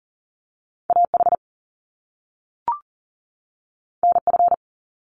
Morse code, heard as a keyed tone, about 41 wpm, 700 Hz. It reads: AH NF
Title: Top Two Letter Combinations in Words 40wpm